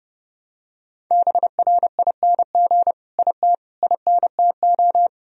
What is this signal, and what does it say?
Morse code, heard as a keyed tone, about 32 wpm, 700 Hz. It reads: BRING IT INTO
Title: Sentences from Top 1000 Words - Review Part 4 - 30wpm